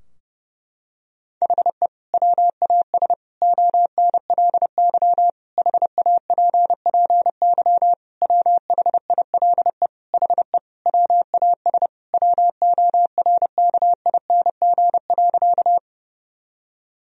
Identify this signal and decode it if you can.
Morse code, heard as a keyed tone, about 30 wpm, 700 Hz. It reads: HE WAS ONLY HAPPY WHILE HE WAS WORKING.
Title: Sentences from Top 1000 Words - Review Part 4 - 30wpm